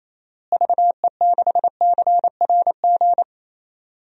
Morse code, heard as a keyed tone, about 27 wpm, 700 Hz. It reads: VE6CRG
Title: DXCC-100 Call Signs - 6 Characters - Repeated 3x - 28wpm